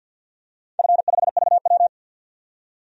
Morse code, heard as a keyed tone, about 51 wpm, 700 Hz. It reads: F5VW
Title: DXCC-100 Call Signs - 4 Characters - 50wpm